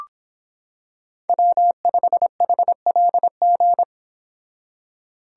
Morse code, heard as a keyed tone, about 26 wpm, 700 Hz. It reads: W5HLG